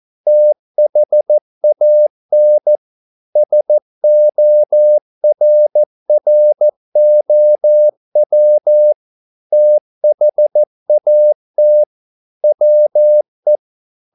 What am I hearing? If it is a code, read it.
Morse code, heard as a keyed tone, about 14 wpm, 600 Hz. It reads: THAN SORROW THAT WE
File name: mp3 files / a dogs mistake - 14wpm